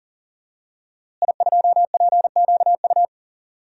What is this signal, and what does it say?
Morse code, heard as a keyed tone, about 40 wpm, 700 Hz. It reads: I2PQU
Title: DXCC-100 Call Signs - 5 Characters - Repeated 3x - 40wpm